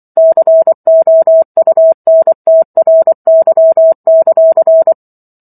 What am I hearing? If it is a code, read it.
Morse code, heard as a keyed tone, about 24 wpm, 650 Hz. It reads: COUNTRY;
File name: Book One / 15-Chapter 15 - 24 WPM